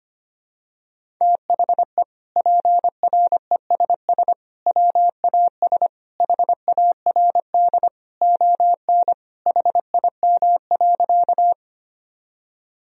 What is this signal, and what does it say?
Morse code, heard as a keyed tone, about 25 wpm, 700 Hz. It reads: THE PRESS WAS HARD ON HIM.
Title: Sentences from Top 1000 Words - Review Part 1 - 25wpm